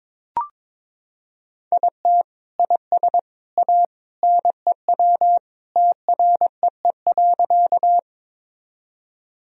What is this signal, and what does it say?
Morse code, heard as a keyed tone, about 22 wpm, 700 Hz. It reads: IT IS A NEW TREE.
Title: Sentences from Top 1000 Words - Review Part 1 - 22wpm